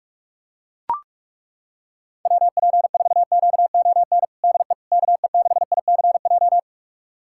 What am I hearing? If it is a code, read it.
Morse code, heard as a keyed tone, about 41 wpm, 700 Hz. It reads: WP4QON DE KE6ICJ
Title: USA Callsign DE Callsign 45wpm